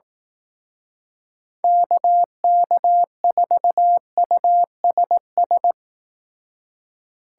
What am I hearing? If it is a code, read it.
Morse code, heard as a keyed tone, about 18 wpm, 700 Hz. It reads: KK4USS
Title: Speed Racing - USA 2x3 Call Signs 15wpm